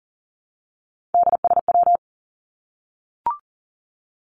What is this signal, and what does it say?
Morse code, heard as a keyed tone, about 40 wpm, 700 Hz. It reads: DSW
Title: Abbreviations 40wpm